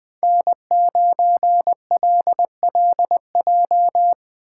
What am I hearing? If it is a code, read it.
Morse code, heard as a keyed tone, about 21 wpm, 700 Hz. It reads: N9LLJ